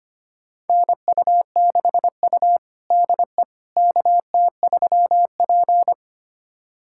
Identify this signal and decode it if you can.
Morse code, heard as a keyed tone, about 25 wpm, 700 Hz. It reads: NU6U DE KT3P
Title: USA Callsign DE Callsign - 4 Characters - 25wpm